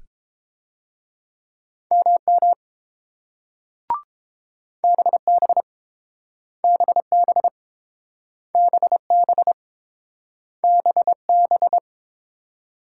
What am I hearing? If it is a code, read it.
Morse code, heard as a keyed tone, about 31 wpm, 700 Hz. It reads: MM BB BB BB BB